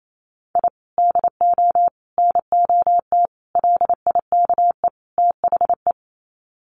Morse code, heard as a keyed tone, about 28 wpm, 700 Hz. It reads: I DO NOT LIKE THE
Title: Sets of 5 Words Encore 28wpm